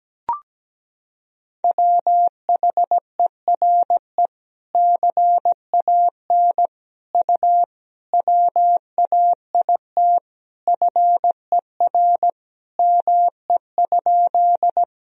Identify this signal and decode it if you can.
Morse code, heard as a keyed tone, about 17 wpm, 700 Hz. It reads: WHERE CAN U WAIT FER ME?
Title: Sentences from Top 500 Words and Abbreviations 17wpm